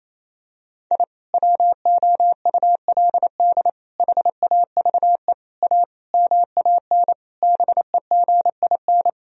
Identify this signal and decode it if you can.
Morse code, heard as a keyed tone, about 28 wpm, 700 Hz. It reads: I WOULD HAVE A MAN BEGIN
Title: Sentences from Top 1000 Words - Review Part 2 - 28wpm